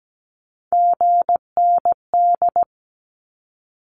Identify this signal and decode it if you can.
Morse code, heard as a keyed tone, about 17 wpm, 700 Hz. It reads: GND
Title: Abbreviations Repeated 3x - 17wpm